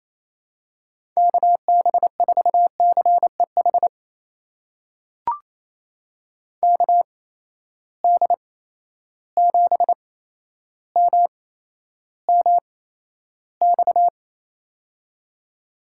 Morse code, heard as a keyed tone, about 28 wpm, spaced as for 12 wpm, 700 Hz. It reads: KB4CEH KD7MMX
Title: USA Call Signs - 6 Characters - 8x Character Spacing - 28wpm